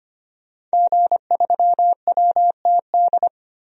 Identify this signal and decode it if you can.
Morse code, heard as a keyed tone, about 25 wpm, 700 Hz. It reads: G3WTD
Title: DXCC-100 Call Signs - 5 Characters - 25wpm